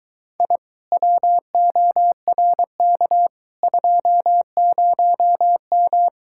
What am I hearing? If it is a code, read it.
Morse code, heard as a keyed tone, about 23 wpm, 700 Hz. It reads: I WORK 20M